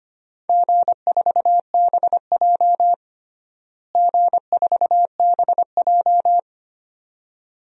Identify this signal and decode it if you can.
Morse code, heard as a keyed tone, about 25 wpm, 700 Hz. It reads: G4BJ G4BJ
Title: DXCC-100 Call Signs - 4 Characters - Repeated 2x - 25wpm